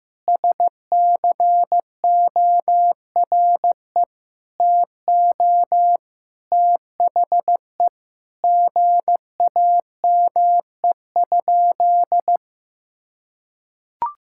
Morse code, heard as a keyed tone, about 15 wpm, 700 Hz. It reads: SCORE TO THE GAME?
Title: Sentences from Top 1600 Words 15wpm